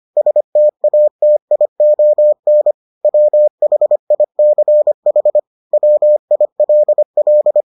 Morse code, heard as a keyed tone, about 25 wpm, 600 Hz. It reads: STATION WHICH WILL